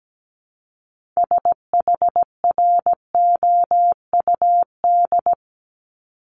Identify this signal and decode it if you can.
Morse code, heard as a keyed tone, about 17 wpm, 700 Hz. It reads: SHROUD
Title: Multi-letter Phonemes 17wpm